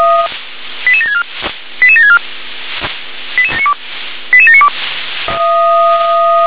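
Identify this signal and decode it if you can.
Five-tone selective call (ZVEI), heard as quick runs of five tones, tones of about 70 ms. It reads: 88764 88764 88792
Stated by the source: ZVEI five-tone selective call received via SDR.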